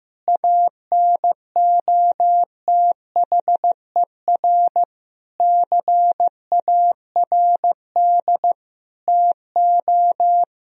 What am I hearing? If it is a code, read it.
Morse code, heard as a keyed tone, about 16 wpm, 700 Hz. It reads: ANOTHER CARD TO
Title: Sentences from Top 1000 Words - Review Part 4 - 15wpm